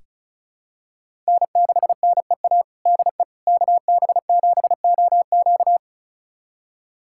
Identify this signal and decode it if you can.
Morse code, heard as a keyed tone, about 36 wpm, 700 Hz. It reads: N6NEA DE KB7OQ
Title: USA Callsign DE Callsign - 5 Characters - 35wpm